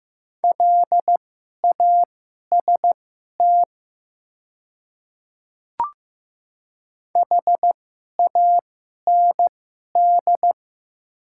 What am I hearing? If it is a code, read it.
Morse code, heard as a keyed tone, about 15 wpm, spaced as for 11 wpm, 700 Hz. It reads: LAST HAND
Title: 2x Character Spacing - Top 300 Words - 15wpm